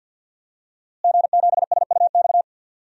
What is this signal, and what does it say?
Morse code, heard as a keyed tone, about 46 wpm, 700 Hz. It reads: G7SUX
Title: DXCC-100 Call Signs - 5 Characters - Repeated 2x - 50wpm